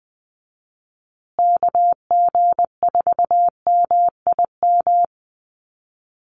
Morse code, heard as a keyed tone, about 20 wpm, 700 Hz. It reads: KG4MIM
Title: USA Call Signs Repeated 3x 20wpm